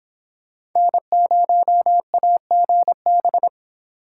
Morse code, heard as a keyed tone, about 23 wpm, 700 Hz. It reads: N0AGB